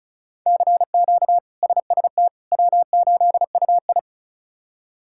Morse code, heard as a keyed tone, about 34 wpm, 700 Hz. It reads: CQ SST W8UI
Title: K1USN SST QSO Practice - 35wpm